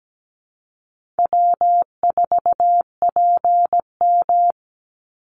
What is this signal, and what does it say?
Morse code, heard as a keyed tone, about 17 wpm, 700 Hz. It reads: W4PM
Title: Easy Call Signs Repeated 3x - 17wpm